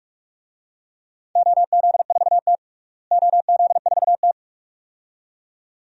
Morse code, heard as a keyed tone, about 46 wpm, 700 Hz. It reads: OZ4T OZ4T
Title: DXCC-100 Call Signs - 4 Characters - Repeated 2x - 45wpm